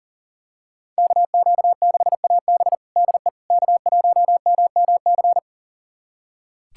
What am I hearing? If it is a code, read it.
Morse code, heard as a keyed tone, about 40 wpm, 700 Hz. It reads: KQ6AB DE K1MMC
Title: USA Callsign DE Callsign - 5 Characters - 40wpm